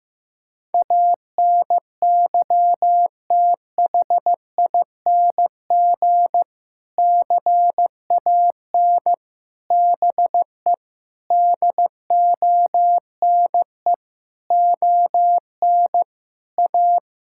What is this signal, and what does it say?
Morse code, heard as a keyed tone, about 16 wpm, 700 Hz. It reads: ANYTHING CAN BE DONE ON A
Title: Sentences from Top 1000 Words - Review Part 5 - 15wpm